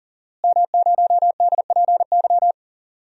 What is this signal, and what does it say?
Morse code, heard as a keyed tone, about 40 wpm, 700 Hz. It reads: M0DPY
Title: DXCC-100 Call Signs - 5 Characters - Repeated 3x - 40wpm